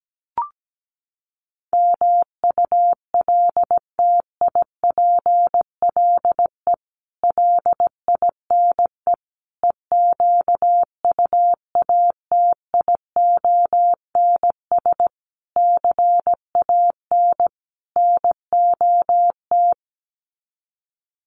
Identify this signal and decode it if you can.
Morse code, heard as a keyed tone, about 17 wpm, 700 Hz. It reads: MULTIPLE LINE EQUATIONS CAN NOT
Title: Sets of 5 Words 17wpm